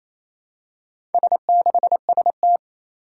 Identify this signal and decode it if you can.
Morse code, heard as a keyed tone, about 27 wpm, 700 Hz. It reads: S6ST
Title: Four Challenging Letter-Number Combinations 28wpm